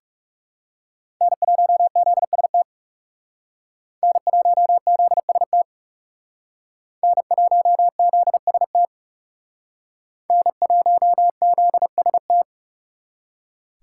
Morse code, heard as a keyed tone, about 37 wpm, 700 Hz. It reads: N1ZST N1ZST N1ZST N1ZST
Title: Speed Racing - USA 1x3 Call Signs 30wpm